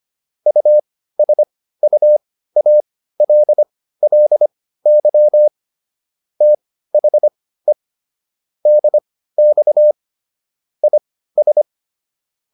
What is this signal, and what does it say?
Morse code, heard as a keyed tone, about 25 wpm, spaced as for 15 wpm, 600 Hz. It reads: USUALLY THE DX IS